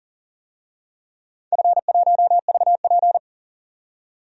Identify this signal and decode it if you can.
Morse code, heard as a keyed tone, about 40 wpm, 700 Hz. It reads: F1VP
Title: DXCC-100 Call Signs - 4 Characters - 40wpm